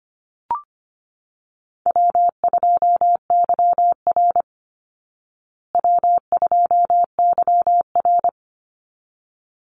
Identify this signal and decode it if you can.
Morse code, heard as a keyed tone, about 25 wpm, 700 Hz. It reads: W2YR W2YR
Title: Easy Call Signs Repeated 2x - 25wpm